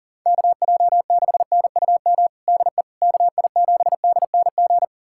Morse code, heard as a keyed tone, about 40 wpm, 700 Hz. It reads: KJ6NUM DE KI7DNG
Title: USA Callsign DE Callsign - 6 Characters - 40wpm